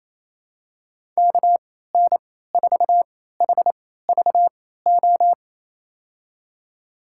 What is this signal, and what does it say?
Morse code, heard as a keyed tone, about 28 wpm, spaced as for 16 wpm, 700 Hz. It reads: KN4HVO
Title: USA Call Signs - 6 Characters - 3x Character Spacing - 28wpm